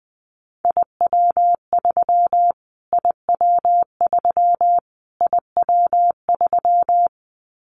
Morse code, heard as a keyed tone, about 20 wpm, 700 Hz. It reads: IW3 IW3 IW3
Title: Callsign Prefixes for Top 25 DXCC - Repeated 3x - 20wpm